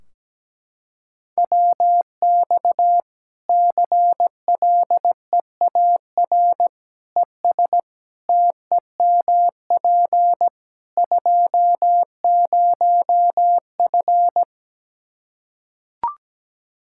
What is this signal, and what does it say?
Morse code, heard as a keyed tone, about 17 wpm, 700 Hz. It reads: WX CLEAR ES TEMP 20F
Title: K1USN SST QSO Practice 17wpm